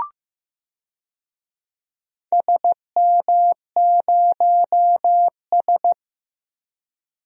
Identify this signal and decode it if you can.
Morse code, heard as a keyed tone, about 15 wpm, 700 Hz. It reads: SM0S
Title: DXCC-100 Call Signs - 4 Characters - Repeated 2x - 15wpm